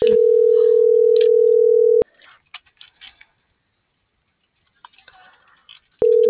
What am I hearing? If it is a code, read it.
Ringback tone (North America), 440 + 480 Hz.